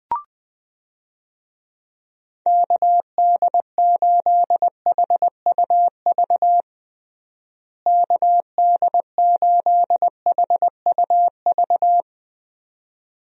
Morse code, heard as a keyed tone, about 21 wpm, 700 Hz. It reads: KD8HUV KD8HUV
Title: USA Call Signs - 6 Characters - Repeated 2x - 20wpm